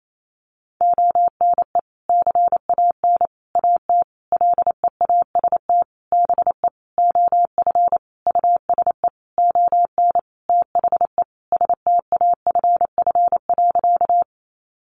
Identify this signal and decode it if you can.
Morse code, heard as a keyed tone, about 28 wpm, 700 Hz. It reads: ONE CAN AT LEAST BE OF USE ON THE STAFF.
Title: Sentences from Top 1200 Words 28wpm